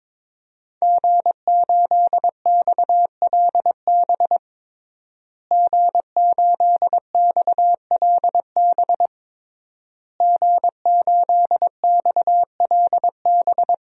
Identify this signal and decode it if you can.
Morse code, heard as a keyed tone, about 21 wpm, 700 Hz. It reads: G8XLB G8XLB G8XLB
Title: DXCC-100 Call Signs - 5 Characters - Repeated 3x - 22wpm